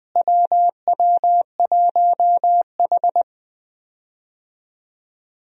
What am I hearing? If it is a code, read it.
Morse code, heard as a keyed tone, about 21 wpm, 700 Hz. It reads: WW1H
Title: USA Call Signs - 4 Characters - Repeated 3x - 20wpm